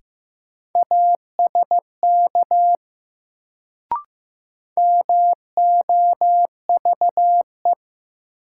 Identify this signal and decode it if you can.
Morse code, heard as a keyed tone, about 16 wpm, 700 Hz. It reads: ASK MOVE
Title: Sentences from Top 200 Words - New Word Drill - 15wpm